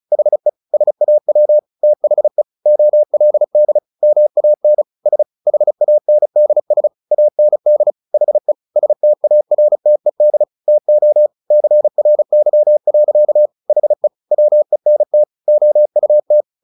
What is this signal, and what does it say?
Morse code, heard as a keyed tone, about 34 wpm, 600 Hz. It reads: HE SAW THE OLD MAN S HANDS AND HE STARTED TO CRY. HE WENT OUT
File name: the old man and the sea 2 35wpm